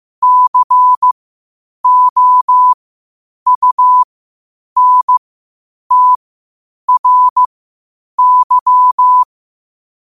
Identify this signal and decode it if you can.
Morse code, heard as a keyed tone, about 15 wpm, spaced as for 8 wpm, 1000 Hz. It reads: COUNTRY
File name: 8wpm